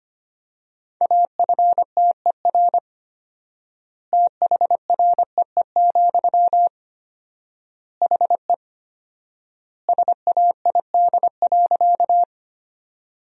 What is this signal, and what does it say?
Morse code, heard as a keyed tone, about 25 wpm, 700 Hz. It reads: AFTER THREE, HE SAID.
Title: 4x Word-Spacing - Sentences from Top 200 Words - 25wpm